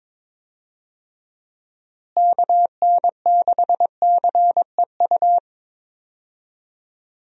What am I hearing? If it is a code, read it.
Morse code, heard as a keyed tone, about 22 wpm, 700 Hz. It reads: KN6CEU
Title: USA Call Signs - 6 Characters - Repeated 2x - 22wpm